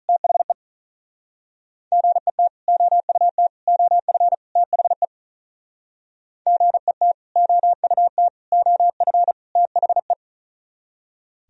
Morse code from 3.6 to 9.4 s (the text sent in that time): OF THE GET OUT OF